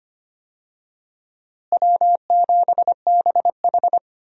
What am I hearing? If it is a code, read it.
Morse code, heard as a keyed tone, about 25 wpm, 700 Hz. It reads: W7BH
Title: USA Call Signs - 4 Characters - Repeated 2x - 25wpm